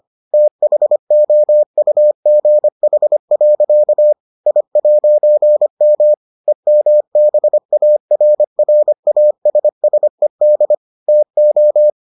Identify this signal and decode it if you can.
Morse code, heard as a keyed tone, about 25 wpm, 600 Hz. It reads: THOUGH. I'M EMBARRASSED TO